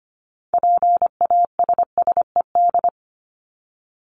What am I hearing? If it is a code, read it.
Morse code, heard as a keyed tone, about 25 wpm, 700 Hz. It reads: PASSED
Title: ED Suffix - Top 100 Words with Base Words 25wpm